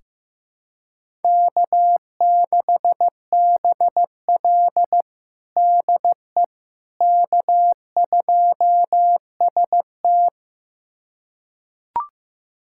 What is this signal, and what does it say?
Morse code, heard as a keyed tone, about 16 wpm, 700 Hz. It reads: K6BL DE K2ST
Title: USA Callsign DE Callsign - 4 Characters - 15wpm